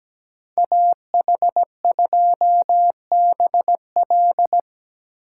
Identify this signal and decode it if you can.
Morse code, heard as a keyed tone, about 19 wpm, 700 Hz. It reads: AH2BL
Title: Speed Racing - USA Call Signs 17wpm